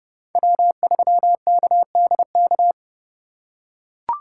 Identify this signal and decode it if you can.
Morse code, heard as a keyed tone, about 30 wpm, 700 Hz. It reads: W3KDK
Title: USA Call Signs - 5 Characters - 3x Character Spacing - 30wpm